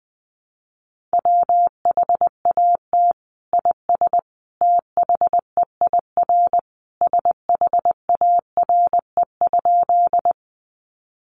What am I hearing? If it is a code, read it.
Morse code, heard as a keyed tone, about 20 wpm, 700 Hz. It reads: WHAT IS THEIR SHARE?
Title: Sentences from Top 900 Words 20wpm